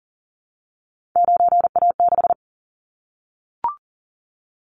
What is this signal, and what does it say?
Morse code, heard as a keyed tone, about 38 wpm, 700 Hz. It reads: 9A6
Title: Callsign Prefixes for Top 100 DXCC - 40wpm